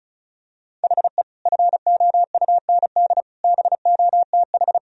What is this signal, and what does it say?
Morse code, heard as a keyed tone, about 36 wpm, 700 Hz. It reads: HE FOUND BOTH
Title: Sentences from Top 1000 Words - Review Part 1 - 35wpm